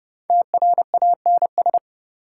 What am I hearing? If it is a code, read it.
Morse code, heard as a keyed tone, about 29 wpm, 700 Hz. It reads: TRANS
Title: Prefixes and Suffixes 30wpm